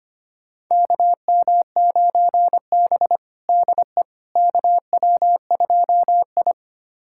Morse code, heard as a keyed tone, about 25 wpm, 700 Hz. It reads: KM9B DE KW2I
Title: USA Callsign DE Callsign - 4 Characters - 25wpm